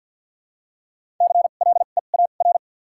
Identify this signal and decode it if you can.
Morse code, heard as a keyed tone, about 53 wpm, spaced as for 37 wpm, 700 Hz. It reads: CLEAR
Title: 2x Character Spacing - Top 500 Words - 50wpm